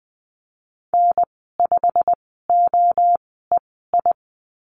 Morse code, heard as a keyed tone, about 20 wpm, spaced as for 14 wpm, 700 Hz. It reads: N5OEI
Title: 088 Lesson 15g - Number 5 - Call Signs 20wpm